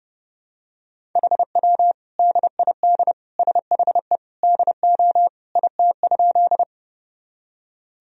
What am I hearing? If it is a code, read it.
Morse code, heard as a keyed tone, about 30 wpm, 700 Hz. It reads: HW DID SHE DO IT?
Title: Sentences from Top 100 Words and Abbreviations 30wpm